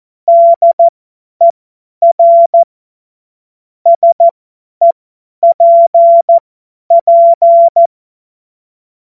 Morse code, heard as a keyed tone, about 14 wpm, spaced as for 10 wpm, 680 Hz. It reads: DER SEPP
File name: Witz-09-September-2025-50-F70